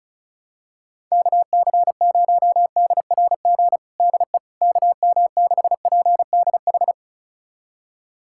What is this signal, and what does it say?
Morse code, heard as a keyed tone, about 36 wpm, 700 Hz. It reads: KC0DRG DE KM6PDH
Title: USA Callsign DE Callsign - 6 Characters - 35wpm